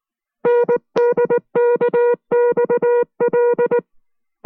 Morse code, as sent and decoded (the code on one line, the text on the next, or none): -. -.. -.- -..- .-..
NDKXL